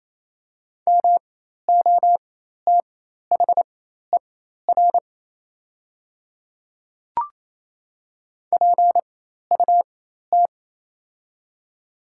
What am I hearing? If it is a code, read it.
Morse code, heard as a keyed tone, about 28 wpm, spaced as for 13 wpm, 700 Hz. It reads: MOTHER PUT
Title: 4x Character Spacing - Top 300 Words - 28wpm